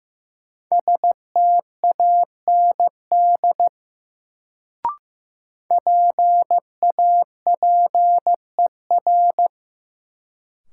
Morse code, heard as a keyed tone, about 15 wpm, 700 Hz. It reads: STAND PAPER
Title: Sentences from Top 300 Words - New Word Drill - 15wpm